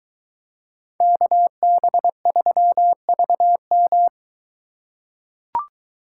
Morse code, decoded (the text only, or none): KB3VM